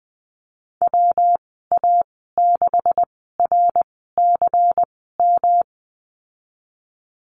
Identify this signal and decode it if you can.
Morse code, heard as a keyed tone, about 20 wpm, spaced as for 15 wpm, 700 Hz. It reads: WA6RCM
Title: 154 Lesson 25g - Letter p - Call Signs 20wpm